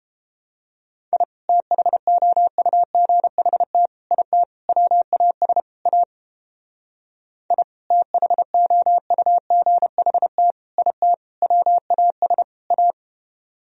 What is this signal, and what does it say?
Morse code, heard as a keyed tone, about 31 wpm, 700 Hz. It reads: I THOUGHT IT WAS A I THOUGHT IT WAS A